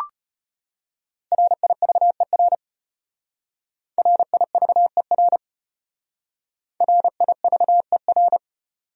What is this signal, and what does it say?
Morse code, heard as a keyed tone, about 30 wpm, 700 Hz. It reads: RIVER RIVER RIVER
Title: Speed-Racing Top 300 Words 25wpm